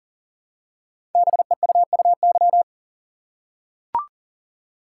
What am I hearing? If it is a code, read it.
Morse code, heard as a keyed tone, about 38 wpm, 690 Hz. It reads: BEUUY
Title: Code Groups - 5 Letters - 40wpm